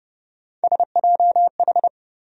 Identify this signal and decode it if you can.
Morse code, heard as a keyed tone, about 32 wpm, 700 Hz. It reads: SJH
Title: USA Call Sign Postfixes - 3 Letters 30wpm